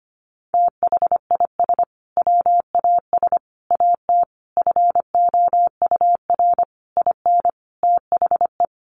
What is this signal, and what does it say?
Morse code, heard as a keyed tone, about 25 wpm, 700 Hz. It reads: THIS WAS AT FOUR IN THE
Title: Sentences from Top 1500 Words 25wpm